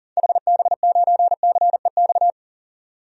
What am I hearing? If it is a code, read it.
Morse code, heard as a keyed tone, about 40 wpm, 700 Hz. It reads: HB9CEX
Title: DXCC-100 Call Signs - 6 Characters - 40wpm